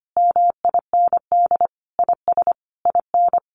Morse code, heard as a keyed tone, about 25 wpm, 700 Hz. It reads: MIND IS IN
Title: Sentences from Top 1500 Words 25wpm